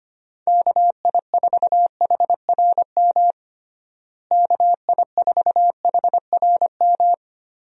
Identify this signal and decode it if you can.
Morse code, heard as a keyed tone, about 25 wpm, 700 Hz. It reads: KI4HRM KI4HRM
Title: USA Call Signs - 6 Characters - Repeated 3x - 25wpm